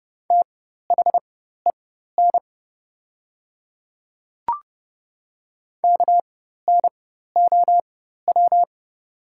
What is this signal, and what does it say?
Morse code, heard as a keyed tone, about 30 wpm, spaced as for 14 wpm, 700 Hz. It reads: THEN KNOW
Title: 4x Character Spacing - Top 100 Words - 30wpm